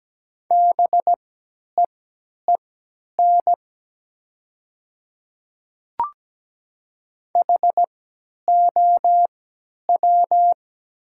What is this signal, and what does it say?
Morse code, heard as a keyed tone, about 17 wpm, spaced as for 10 wpm, 700 Hz. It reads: BEEN HOW
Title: 3x Character Spacing - Top 100 Words - 17wpm